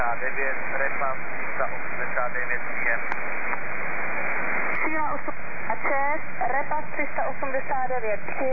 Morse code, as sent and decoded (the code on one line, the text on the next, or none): none